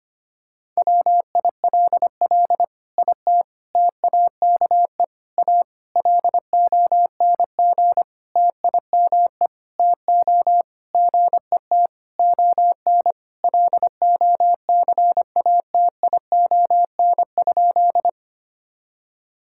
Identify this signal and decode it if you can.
Morse code, heard as a keyed tone, about 25 wpm, 700 Hz. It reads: WILL IT TAKE A LONG TIME TO GET ON LOCATION?
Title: Sentences from Top 1000 Words - Review Part 6 - 25wpm